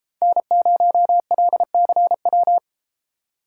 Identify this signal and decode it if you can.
Morse code, heard as a keyed tone, about 30 wpm, 700 Hz. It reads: N0LCW